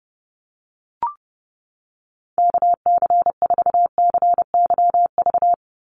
Morse code, heard as a keyed tone, about 30 wpm, 700 Hz. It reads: KC4CYV
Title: USA Call Signs 30wpm